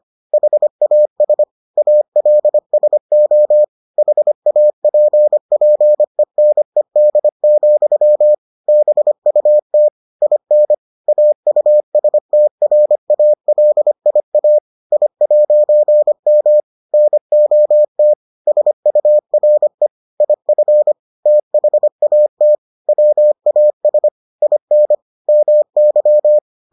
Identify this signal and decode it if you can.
Morse code, heard as a keyed tone, about 25 wpm, 600 Hz. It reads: HAS ALSO HAPPENED, BUT IN AUSTRALIA I'M NOT SURE IF THAT WAS IN MY